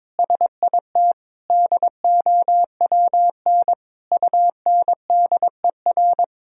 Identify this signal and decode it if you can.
Morse code, heard as a keyed tone, about 22 wpm, 700 Hz. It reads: SIT DOWN UNDER
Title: Sentences from Top 1000 Words - Review Part 1 - 22wpm